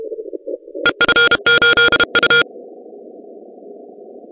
Morse code, heard as a keyed tone, about 32 wpm, 1420 Hz. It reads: EF8U
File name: 2015 cq ww cw / 2015 cqwww k5tr - 0-4